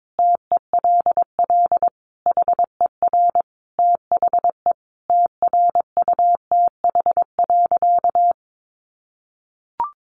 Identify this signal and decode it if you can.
Morse code, heard as a keyed tone, about 22 wpm, 700 Hz. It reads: TELL HER THE TRUTH.
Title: Sentences from Top 800 Words 22wpm